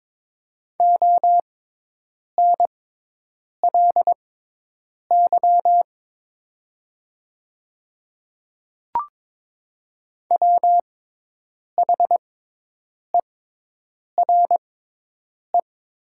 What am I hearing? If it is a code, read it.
Morse code, heard as a keyed tone, about 22 wpm, spaced as for 8 wpm, 700 Hz. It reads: ONLY WHERE
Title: 6x Character Spacing - Top 200 Words - 22wpm